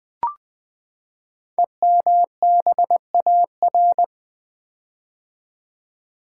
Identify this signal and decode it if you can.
Morse code, heard as a keyed tone, about 20 wpm, 700 Hz. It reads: EMBAR
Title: Build A Word - 5 Letters At A Time - 12-Letter Words - 20wpm